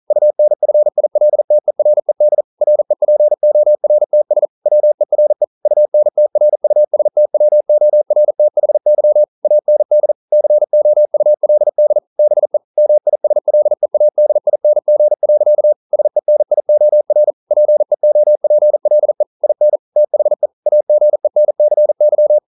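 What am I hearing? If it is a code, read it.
Morse code, heard as a keyed tone, about 42 wpm, 600 Hz. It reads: UNFILTERED REPORTS WERE UNTRUSTWORTHY AND COULD BE MISLEADING. SENIOR PEOPLE IN THE AGENCY
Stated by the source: Morse code practice at 41 wpm, from CW med Gnister.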